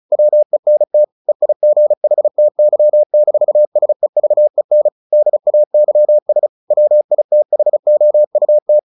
Morse code, heard as a keyed tone, about 34 wpm, 600 Hz. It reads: WENT EIGHTY-SEVEN DAYS WITHOUT
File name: the old man and the sea / the old man and the sea 1 35wpm